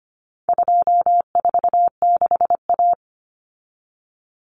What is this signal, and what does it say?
Morse code, heard as a keyed tone, about 25 wpm, 700 Hz. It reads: 246A
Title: Common Cut Numbers - 4-Digit 25wpm